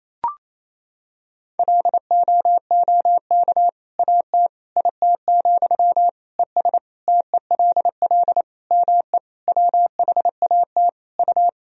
Morse code, heard as a keyed tone, about 28 wpm, 700 Hz. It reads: LOOK AT IT, ES TELL ME WHAT U
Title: Sentences from Top 200 Words and Abbreviations 28wpm